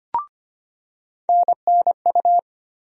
Morse code, heard as a keyed tone, about 25 wpm, 700 Hz. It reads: NNU
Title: USA Call Sign Postfixes - 3 Letters 25wpm